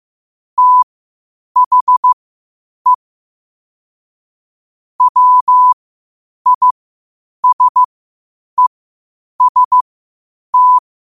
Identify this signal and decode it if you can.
Morse code, heard as a keyed tone, about 15 wpm, spaced as for 8 wpm, 1000 Hz. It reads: THE WISEST
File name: qotdfiles / 8wpm